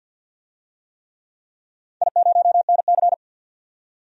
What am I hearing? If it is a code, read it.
Morse code, heard as a keyed tone, about 48 wpm, 700 Hz. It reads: I0NC
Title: DXCC-100 Call Signs - 4 Characters - 50wpm